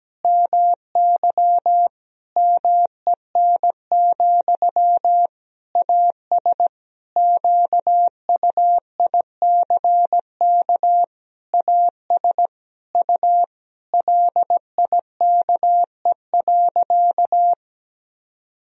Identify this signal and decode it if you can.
Morse code, heard as a keyed tone, about 17 wpm, 700 Hz. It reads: MY MEN, AS QUICK AS U LIKE.
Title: Sentences from Top 500 Words and Abbreviations 17wpm